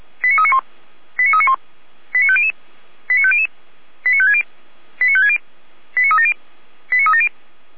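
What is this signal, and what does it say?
Five-tone selective call (ZVEI), heard as quick runs of five tones, tones of about 80 ms. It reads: 89391 89391 89500 89500 89560 89560 89370 89370
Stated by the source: ZVEI five-tone call 89338 received by SDR.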